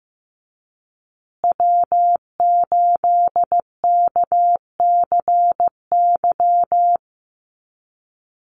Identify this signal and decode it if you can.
Morse code, heard as a keyed tone, about 15 wpm, 700 Hz. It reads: W8KCY
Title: USA Call Signs Repeated 2x 15wpm